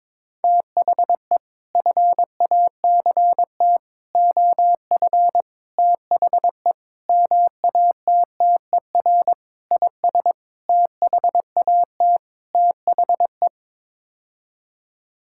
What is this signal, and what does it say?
Morse code, heard as a keyed tone, about 22 wpm, 700 Hz. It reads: THE FACT OF THE MATTER IS THAT THE
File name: practice / 22wpm / Sets of 8 Words - 22wpm